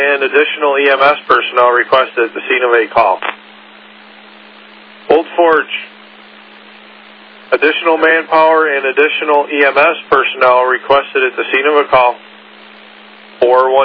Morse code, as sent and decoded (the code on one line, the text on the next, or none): none